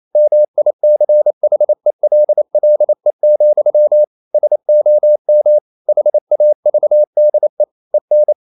Morse code, heard as a keyed tone, about 28 wpm, 600 Hz. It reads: MICHELLE, SOM HAVDE EN
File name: Ferie QRQ 28wpm